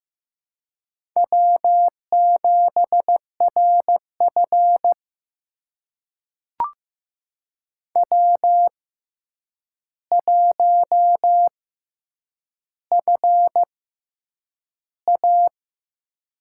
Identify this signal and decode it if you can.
Morse code, heard as a keyed tone, about 15 wpm, spaced as for 8 wpm, 700 Hz. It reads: W7RF W1FA
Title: USA Call Signs - 4 Characters - 6x Character Spacing - 15wpm